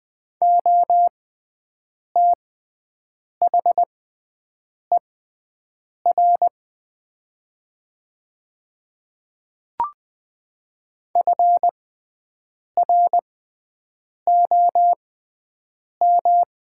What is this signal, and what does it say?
Morse code, heard as a keyed tone, about 20 wpm, spaced as for 7 wpm, 700 Hz. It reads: OTHER FROM
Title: 6x Character Spacing - Top 200 Words - 20wpm